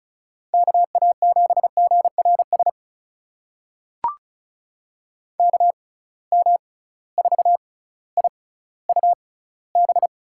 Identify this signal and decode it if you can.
Morse code, heard as a keyed tone, about 34 wpm, spaced as for 18 wpm, 700 Hz. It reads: KA7GRS KM4IUB
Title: USA Call Signs - 6 Characters - 6x Character Spacing - 35wpm